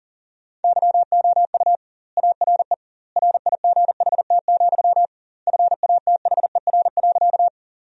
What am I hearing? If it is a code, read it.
Morse code, heard as a keyed tone, about 40 wpm, 700 Hz. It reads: YOU ARE RIGHT, FATHER.
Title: 2x Word-Spacing - Sentences from Top 300 Words - 40wpm